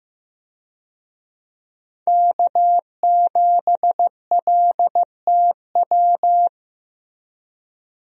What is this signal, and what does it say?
Morse code, heard as a keyed tone, about 15 wpm, 700 Hz. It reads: K7LTW
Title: USA Call Signs - 5 Characters - Repeated 2x - 15wpm